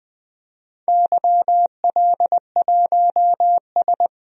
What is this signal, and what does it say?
Morse code, heard as a keyed tone, about 20 wpm, 700 Hz. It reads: YL1S
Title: DXCC-100 Call Signs - 4 Characters - 20wpm